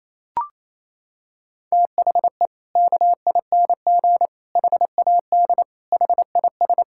Morse code, heard as a keyed tone, about 28 wpm, 700 Hz. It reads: THE KING HAD HIS
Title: Sentences from Top 1000 Words - Review Part 6 - 28wpm